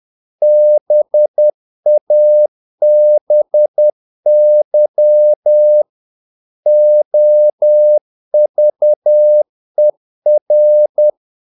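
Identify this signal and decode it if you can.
Morse code, heard as a keyed tone, about 10 wpm, 600 Hz. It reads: BABY OVER
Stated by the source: Morse practice of short Danish words at 10 wpm, from CW med Gnister.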